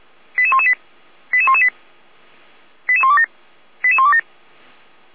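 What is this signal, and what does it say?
Five-tone selective call (ZVEI), heard as quick runs of five tones, tones of about 70 ms. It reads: 88118 88118 88127 88127
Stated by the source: ZVEI five-tone call received on SDR, Feuerwehr Orlamünde.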